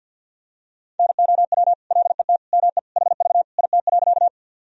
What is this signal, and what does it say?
Morse code, heard as a keyed tone, about 52 wpm, 700 Hz. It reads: NOW LET ME HV IT.
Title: Sentences from Top 300 Words and Abbreviations 50wpm